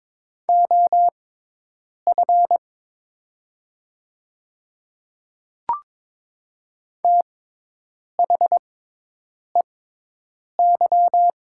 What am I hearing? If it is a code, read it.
Morse code, heard as a keyed tone, about 22 wpm, spaced as for 8 wpm, 700 Hz. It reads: OF THEY
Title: 6x Character Spacing - Top 100 Words - 22wpm